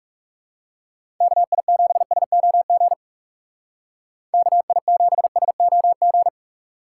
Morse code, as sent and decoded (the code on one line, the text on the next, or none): -.- .. --... ... --- --. / -.- .. --... ... --- --.
KI7SOG KI7SOG